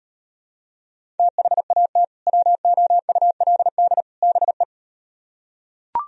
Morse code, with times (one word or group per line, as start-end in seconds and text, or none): 1.2-2.0 THAT
2.3-4.0 WOULD
4.2-4.6 BE